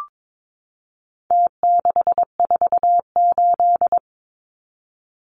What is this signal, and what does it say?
Morse code, heard as a keyed tone, about 22 wpm, 700 Hz. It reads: T648
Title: Numbers - Contest Style - 4-Digit 22wpm